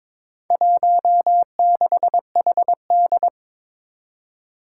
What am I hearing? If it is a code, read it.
Morse code, heard as a keyed tone, about 22 wpm, 700 Hz. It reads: 16HD
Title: Four Challenging Letter-Number Combinations 22wpm